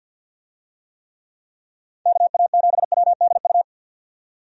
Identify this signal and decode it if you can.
Morse code, heard as a keyed tone, about 48 wpm, 700 Hz. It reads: KA7WDU
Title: USA Call Signs - 6 Characters - 50wpm